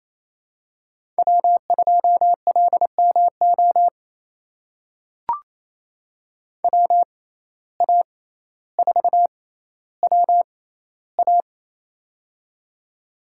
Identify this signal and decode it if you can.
Morse code, heard as a keyed tone, about 28 wpm, spaced as for 14 wpm, 700 Hz. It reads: W2LMO WA4WA
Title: USA Call Signs - 5 Characters - 6x Character Spacing - 28wpm